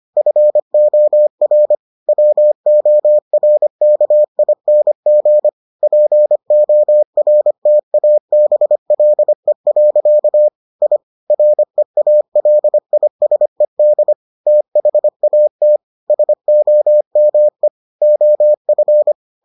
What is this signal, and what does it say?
Morse code, heard as a keyed tone, about 25 wpm, 600 Hz. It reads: FOR WORKING PORTABLE. I REALISED THAT SOME OF